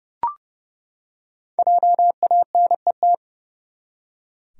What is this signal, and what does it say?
Morse code, heard as a keyed tone, about 30 wpm, 700 Hz. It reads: JANET
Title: USA Common Names - Max 5 Letters - 30wpm